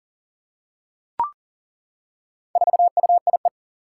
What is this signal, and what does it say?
Morse code, heard as a keyed tone, about 40 wpm, 700 Hz. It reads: 4UIE
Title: Four Challenging Letter-Number Combinations 40wpm